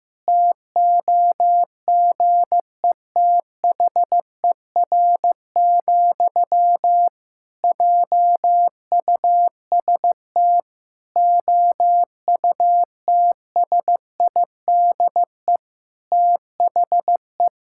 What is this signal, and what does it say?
Morse code, heard as a keyed tone, about 15 wpm, 700 Hz. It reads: TOGETHER, JUST OUTSIDE THE
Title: Sentences from Top 1000 Words - Review Part 6 - 15wpm